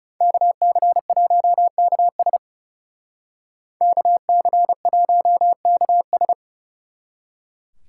Morse code, heard as a keyed tone, about 39 wpm, 700 Hz. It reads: KC1KS KC1KS